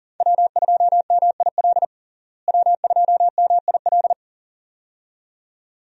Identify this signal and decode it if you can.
Morse code, heard as a keyed tone, about 40 wpm, 700 Hz. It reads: W2MIL W2MIL
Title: USA Call Signs - 5 Characters - Repeated 2x - 40wpm